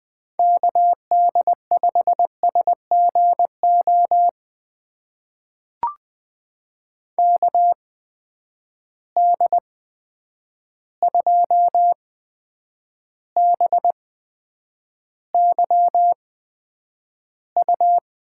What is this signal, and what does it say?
Morse code, heard as a keyed tone, about 20 wpm, spaced as for 9 wpm, 700 Hz. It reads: KD5SGO KD2BYU
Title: USA Call Signs - 6 Characters - 8x Character Spacing - 20wpm